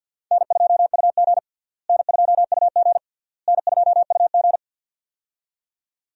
Morse code, heard as a keyed tone, about 50 wpm, 700 Hz. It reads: N2UG N2UG N2UG
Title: USA Call Signs - 4 Characters - Repeated 3x - 50wpm